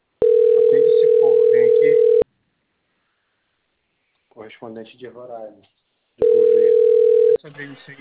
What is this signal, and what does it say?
Ringback tone (North America), 440 + 480 Hz, with speech.
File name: rg-1600-unknown- - - .34749